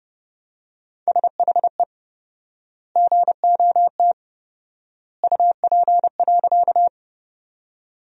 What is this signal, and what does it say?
Morse code, heard as a keyed tone, about 31 wpm, 700 Hz. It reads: SHE GOT UP.
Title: 4x Word-Spacing - Sentences from Top 300 Words - 30wpm